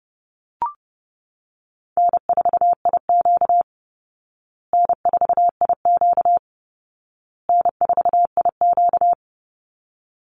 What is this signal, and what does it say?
Morse code, heard as a keyed tone, about 30 wpm, 700 Hz. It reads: N4IQ N4IQ N4IQ
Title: Easy Call Signs Repeated 3x - 30wpm